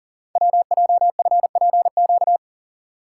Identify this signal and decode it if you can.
Morse code, heard as a keyed tone, about 40 wpm, 700 Hz. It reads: WJFPQ
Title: Code Groups - 5 Letters - 40wpm